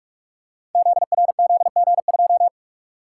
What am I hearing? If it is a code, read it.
Morse code, heard as a keyed tone, about 44 wpm, 700 Hz. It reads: ZRZG2
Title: Code Groups - 5 Letter:Numbers - 45wpm